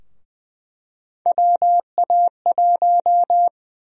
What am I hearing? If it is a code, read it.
Morse code, heard as a keyed tone, about 20 wpm, 700 Hz. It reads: WA1
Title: USA Call Sign Prefixes - 20wpm